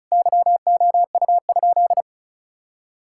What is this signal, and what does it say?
Morse code, heard as a keyed tone, about 36 wpm, 700 Hz. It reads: YOU?
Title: Sentences from Top 1000 Words - Review Part 6 - 35wpm